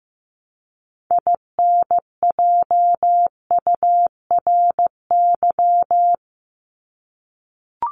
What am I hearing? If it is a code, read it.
Morse code, heard as a keyed tone, about 15 wpm, 700 Hz. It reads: INJURY
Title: Top 1751-2000 Words 15wpm